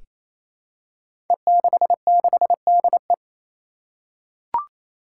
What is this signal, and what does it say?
Morse code, heard as a keyed tone, about 27 wpm, 700 Hz. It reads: E66DE
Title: Five Challenging Letter-Number Combinations 28wpm